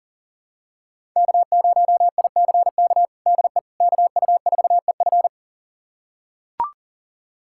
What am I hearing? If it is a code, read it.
Morse code, heard as a keyed tone, about 37 wpm, 700 Hz. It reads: K0ICK DE KU4EF
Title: USA Callsign DE Callsign - 5 Characters - 40wpm